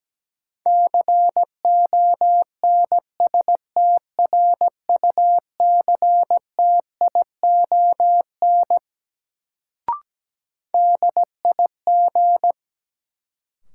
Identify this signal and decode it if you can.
Morse code, heard as a keyed tone, about 17 wpm, 700 Hz. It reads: CONSTRUCTION DIG
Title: Sentences from Top 1600 Words - New Word Drill - 17wpm